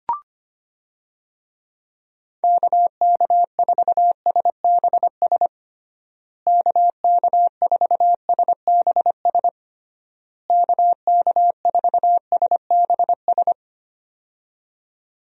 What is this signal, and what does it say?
Morse code, heard as a keyed tone, about 25 wpm, 700 Hz. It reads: KK4SBS KK4SBS KK4SBS
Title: USA Call Signs - 6 Characters - Repeated 3x - 25wpm